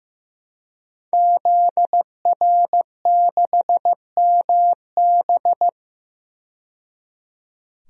Morse code, heard as a keyed tone, about 15 wpm, 700 Hz. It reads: ZR6MB
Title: DXCC-100 Call Signs - 5 Characters - 15wpm